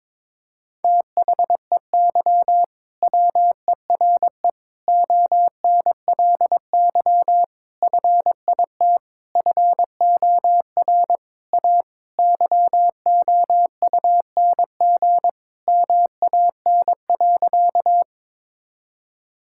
Morse code, heard as a keyed tone, about 22 wpm, 700 Hz. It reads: THEY WERE ONLY FIT FOR A YOUNG MAN.
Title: Sentences from Top 1000 Words - Review Part 5 - 22wpm